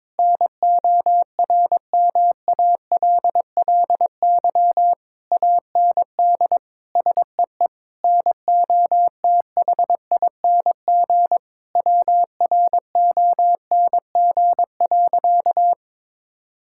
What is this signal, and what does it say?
Morse code, heard as a keyed tone, about 22 wpm, 700 Hz. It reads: NORMALLY AND SEE NOTHING WRONG.
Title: Sentences from Top 1000 Words - Review Part 6 - 22wpm